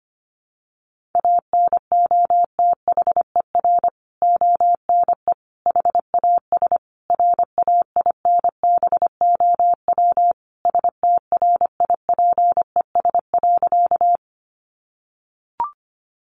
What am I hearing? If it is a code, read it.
Morse code, heard as a keyed tone, about 25 wpm, 700 Hz. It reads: ANOTHER ONE HAS RAINBOW STRIPES.
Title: Additional Easy Phrases 25wpm